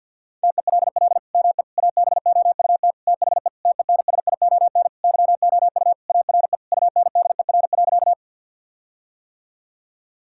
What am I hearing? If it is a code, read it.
Morse code, heard as a keyed tone, about 47 wpm, 700 Hz. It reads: TELL ME ABOUT THE TENSION YOU ARE UNDER.
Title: Sentences from Top 1600 Words 50wpm